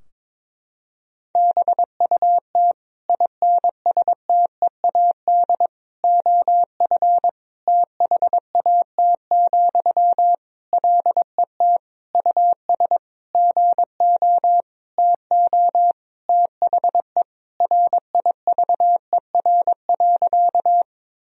Morse code, heard as a keyed tone, about 22 wpm, 700 Hz. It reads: BUT INSTEAD OF THAT, LET US GO TO THE RIVER.
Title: Sentences from Top 1000 Words - Review Part 4 - 22wpm